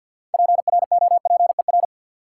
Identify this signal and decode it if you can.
Morse code, heard as a keyed tone, about 52 wpm, 700 Hz. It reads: PROPER
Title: Sentences from Top 800 Words - New Word Drill - 50wpm